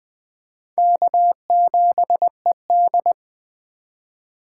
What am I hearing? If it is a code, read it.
Morse code, heard as a keyed tone, about 21 wpm, 700 Hz. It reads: K7ED